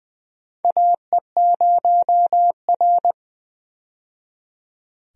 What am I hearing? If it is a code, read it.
Morse code, heard as a keyed tone, about 20 wpm, 700 Hz. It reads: AE0R
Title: USA Call Signs - 4 Characters - 20wpm